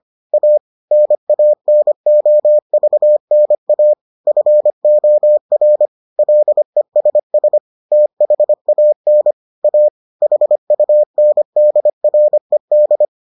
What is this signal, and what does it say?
Morse code, heard as a keyed tone, about 25 wpm, 600 Hz. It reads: A NANOVNA FOR LESS THAN A HUNDRED